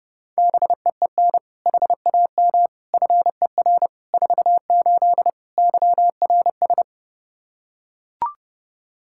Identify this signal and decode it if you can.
Morse code, heard as a keyed tone, about 30 wpm, 700 Hz. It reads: BEEN HAM FER 48 YRS
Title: Practice QSO - Exchange 3 - More Info - 30wpm